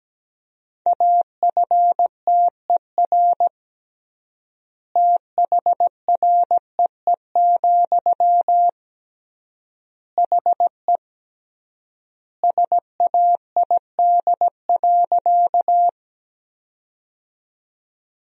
Morse code, heard as a keyed tone, about 17 wpm, 700 Hz. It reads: AFTER THREE, HE SAID.
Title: 3x Word-Spacing - Sentences from Top 200 Words - 17wpm